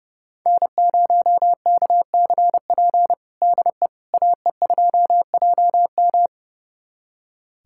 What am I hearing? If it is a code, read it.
Morse code, heard as a keyed tone, about 32 wpm, 700 Hz. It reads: N0KCP DE AE2JM
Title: USA Callsign DE Callsign 30wpm